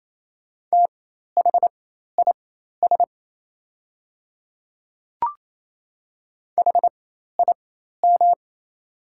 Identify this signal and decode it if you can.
Morse code, heard as a keyed tone, about 28 wpm, spaced as for 13 wpm, 700 Hz. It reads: THIS HIM
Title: 4x Character Spacing - Top 100 Words - 28wpm